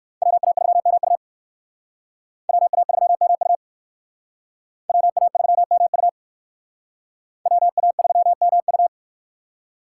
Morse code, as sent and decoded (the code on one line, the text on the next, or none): .-- .- ...-- -- ..- / .-- .- ...-- -- ..- / .-- .- ...-- -- ..- / .-- .- ...-- -- ..-
WA3MU WA3MU WA3MU WA3MU